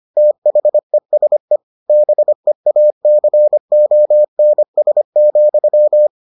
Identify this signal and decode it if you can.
Morse code, heard as a keyed tone, about 25 wpm, 600 Hz. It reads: THESE BEACONS,